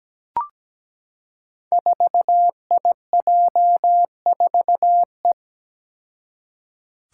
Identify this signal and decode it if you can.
Morse code, heard as a keyed tone, about 17 wpm, 700 Hz. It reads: 4IJ4E
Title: Five Challenging Letter-Number Combinations 17wpm